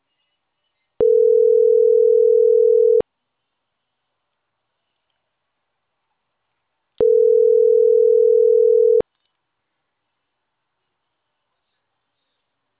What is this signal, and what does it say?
Ringback tone (North America), 440 + 480 Hz.